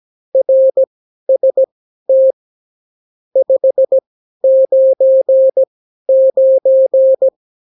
Morse code, heard as a keyed tone, about 17 wpm, spaced as for 12 wpm, 530 Hz. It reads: RST 599
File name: CW QSO libroCW4 18-8ppm